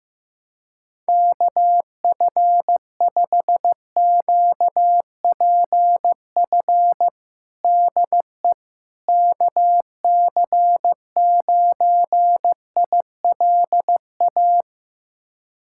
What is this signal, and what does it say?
Morse code, heard as a keyed tone, about 15 wpm, 700 Hz. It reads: KF5QPF DE KC9ILA
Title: USA Callsign DE Callsign 15wpm